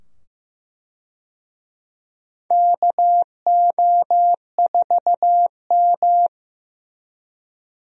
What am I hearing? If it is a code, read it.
Morse code, heard as a keyed tone, about 15 wpm, 700 Hz. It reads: KO4M
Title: USA Call Signs - 4 Characters - 15wpm